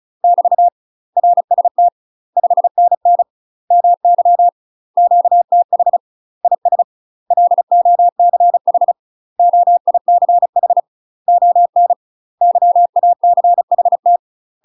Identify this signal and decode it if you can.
Morse code, heard as a keyed tone, about 34 wpm, 700 Hz. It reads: <BT> RST 5NN MY QTH IS LOCH OICH ON YACHT
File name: UK MASTER QSOs TRAINING / 35WPM / UK MASTER QSOs 0014